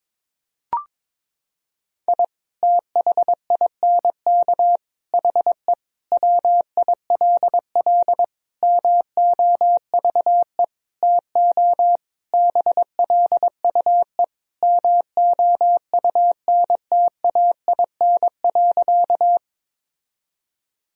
Morse code, heard as a keyed tone, about 22 wpm, 700 Hz. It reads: I THINK HE WILL MOVE TO BLUE MOUNTAIN.
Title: Sentences from Top 1000 Words - Review Part 3 - 22wpm